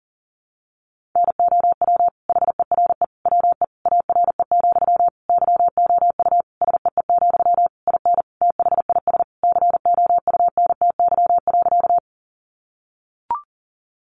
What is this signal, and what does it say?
Morse code, heard as a keyed tone, about 37 wpm, 700 Hz. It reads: NOW HERE WE ARE, YOU SEE, IN THIS COUNTY.
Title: Sentences from Top 1200 Words 40wpm